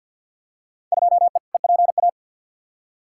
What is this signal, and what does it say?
Morse code, heard as a keyed tone, about 46 wpm, 700 Hz. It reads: 2E EPA
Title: Field Day Exchanges Encore - 50wpm